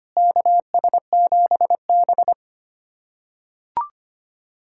Morse code, heard as a keyed tone, about 25 wpm, 700 Hz. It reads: KS7B
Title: USA Callsign DE Callsign - 4 Characters - 25wpm